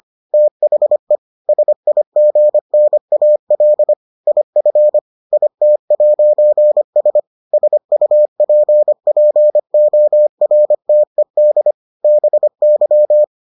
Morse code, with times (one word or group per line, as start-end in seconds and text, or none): 0.3-1.2 THE
1.5-3.9 SIGNAL
4.3-5.0 IF
5.3-7.2 IT'S
7.5-11.7 SUPPORTED
12.0-13.2 BY